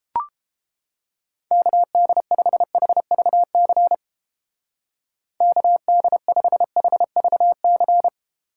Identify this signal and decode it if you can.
Morse code, heard as a keyed tone, about 31 wpm, 700 Hz. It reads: KD5HVC KD5HVC